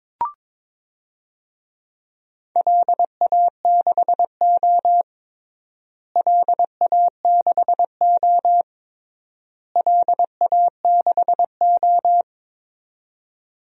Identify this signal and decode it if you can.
Morse code, heard as a keyed tone, about 21 wpm, 700 Hz. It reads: LA6O LA6O LA6O
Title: DXCC-100 Call Signs - 4 Characters - Repeated 3x - 22wpm